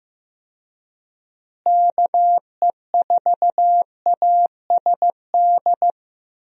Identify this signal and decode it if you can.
Morse code, heard as a keyed tone, about 15 wpm, 700 Hz. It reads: KE4ASD
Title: USA Call Signs - 6 Characters - Repeated 2x - 15wpm